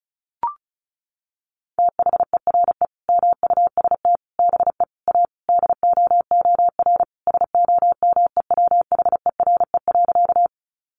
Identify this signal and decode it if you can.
Morse code, heard as a keyed tone, about 34 wpm, 700 Hz. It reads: THERE MUST BE A DOOR SOMEWHERE.
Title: Sentences from Top 1400 Words 35wpm